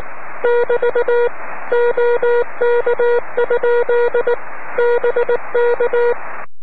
Morse code, heard as a keyed tone, about 19 wpm, 480 Hz. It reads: <BT> OK? BK
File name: om3pv 70mhz / zaznam / ea6vq